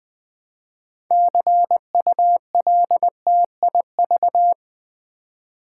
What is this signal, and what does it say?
Morse code, heard as a keyed tone, about 20 wpm, 700 Hz. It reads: CULTIV
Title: Build A Word - 3 Letters At A Time - 9-Letter Words - 20wpm